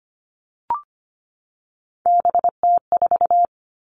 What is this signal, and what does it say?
Morse code, heard as a keyed tone, about 25 wpm, 700 Hz. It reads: BT4
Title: All Cut Numbers - 3-Digit 25wpm